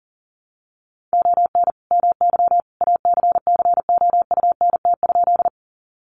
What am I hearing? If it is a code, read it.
Morse code, heard as a keyed tone, about 37 wpm, 700 Hz. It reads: ON MY ACCOUNT?
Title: Sentences from Top 900 Words 40wpm